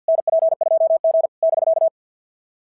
Morse code, heard as a keyed tone, about 46 wpm, 650 Hz. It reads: NP2G <BK>
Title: SOTA QSO Practice 50wpm